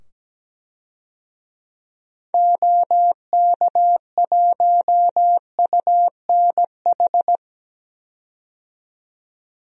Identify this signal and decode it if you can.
Morse code, heard as a keyed tone, about 17 wpm, 700 Hz. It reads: OK1UNH
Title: DXCC-100 Call Signs - 6 Characters - 17wpm